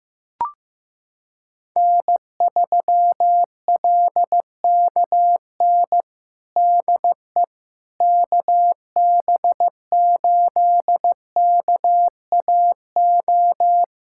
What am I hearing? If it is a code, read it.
Morse code, heard as a keyed tone, about 15 wpm, 700 Hz. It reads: N3LKN DE KB8KAO
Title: USA Callsign DE Callsign 15wpm